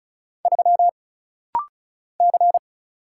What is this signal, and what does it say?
Morse code, heard as a keyed tone, about 35 wpm, 700 Hz. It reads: 3 C
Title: Single Letter-Number Alt - Rapid-Fire 35wpm